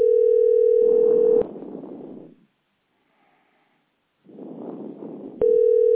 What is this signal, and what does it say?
Ringback tone (North America), 440 + 480 Hz.